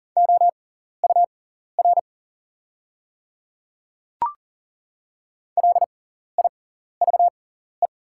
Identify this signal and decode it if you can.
Morse code, heard as a keyed tone, about 40 wpm, spaced as for 14 wpm, 700 Hz. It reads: OUR LIVE
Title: 6x Character Spacing - Top 200 Words - 40wpm